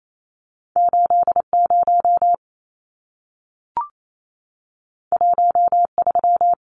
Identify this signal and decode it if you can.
Morse code, heard as a keyed tone, about 28 wpm, 700 Hz. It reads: 80 13
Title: Numbers - 2-Digit 28wpm